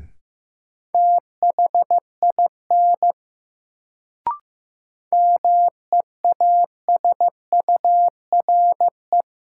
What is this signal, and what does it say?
Morse code, heard as a keyed tone, about 15 wpm, 700 Hz. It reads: THIN MEASURE
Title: Sentences from Top 1200 Words - New Word Drill Alt - 15wpm